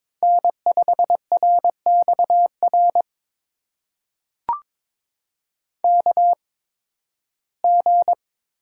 Morse code, heard as a keyed tone, about 22 wpm, spaced as for 14 wpm, 700 Hz. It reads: N5RXR KG
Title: USA Call Signs - 5 Characters - 8x Character Spacing - 22wpm